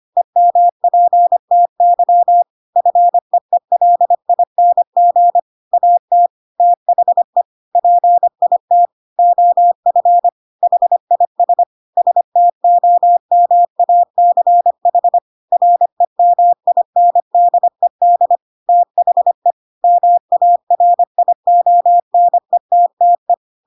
Morse code, heard as a keyed tone, about 25 wpm, 700 Hz. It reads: EMPTY FEELING AT THE PIT OF HIS STOMACH REMINDED THE MARIONETTE
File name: Pinocchio / Pinocchio ch5 25 25 ik1hge